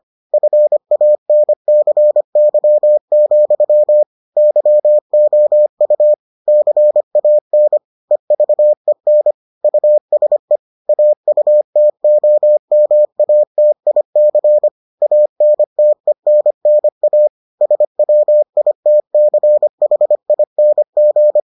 Morse code, heard as a keyed tone, about 25 wpm, 600 Hz. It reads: FANCY, YOU CAN EVEN USE AUTOMATIC ANTENNA SWITCHING